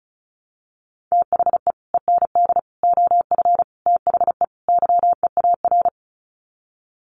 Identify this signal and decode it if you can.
Morse code, heard as a keyed tone, about 34 wpm, 700 Hz. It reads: THE END OF THE YEAR
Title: Sets of 5 Words 35wpm